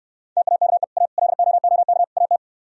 Morse code, heard as a keyed tone, about 69 wpm, 700 Hz. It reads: TAKE A LOOK AT